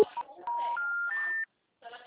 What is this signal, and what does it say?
Special information tone, with speech.
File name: out- -2032- - - .55584